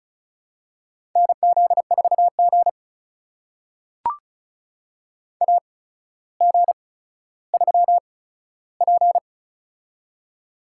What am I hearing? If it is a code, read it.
Morse code, heard as a keyed tone, about 36 wpm, spaced as for 15 wpm, 700 Hz. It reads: NZ4G AG3P
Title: USA Call Signs - 4 Characters - 8x Character Spacing - 35wpm